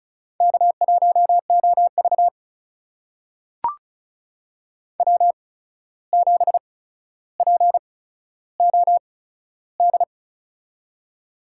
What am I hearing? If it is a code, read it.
Morse code, heard as a keyed tone, about 36 wpm, spaced as for 14 wpm, 700 Hz. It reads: K1OV W7POD
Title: USA Call Signs - 8x Character Spacing - 35wpm